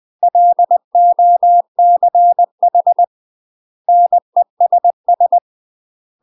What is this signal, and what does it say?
Morse code, heard as a keyed tone, about 20 wpm, 700 Hz. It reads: LOCH NESS
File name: UK MASTER QSOs TRAINING / 20WPM / UK MASTER QSOs 0014